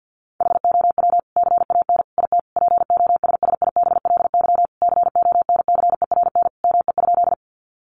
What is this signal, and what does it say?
Morse code, heard as a keyed tone, about 50 wpm, 700 Hz. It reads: HOW CAN IT POSSIBLY CONCERN ME?
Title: Sentences from Top 1300 Words 50wpm